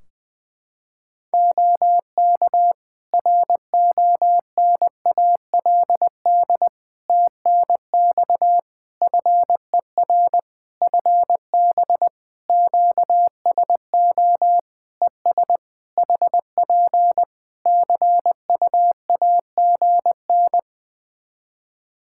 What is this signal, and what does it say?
Morse code, heard as a keyed tone, about 20 wpm, 700 Hz. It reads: OK RONALD TNX FER FB QSO ES HP CUAGN
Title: Practice QSO - Exchange 4 - Ending - 20wpm